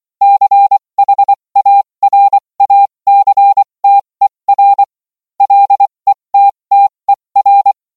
Morse code, heard as a keyed tone, about 24 wpm, spaced as for 20 wpm, 800 Hz. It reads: CHARACTER LETTER